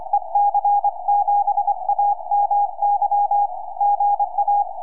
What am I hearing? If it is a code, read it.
Morse code, heard as a keyed tone, about 24 wpm, 820 Hz. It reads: EC7AMY GA
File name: ec7amy 4m